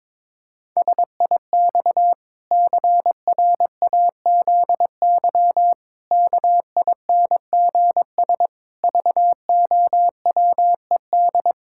Morse code, heard as a keyed tone, about 22 wpm, 700 Hz. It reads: SIX CRAZY KINGS VOWED
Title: Pangram Challenge v2 - 22wpm